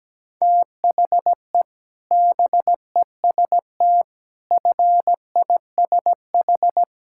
Morse code, heard as a keyed tone, about 17 wpm, 700 Hz. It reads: THE BEST FISH
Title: Sentences from Top 1000 Words - Review Part 6 - 17wpm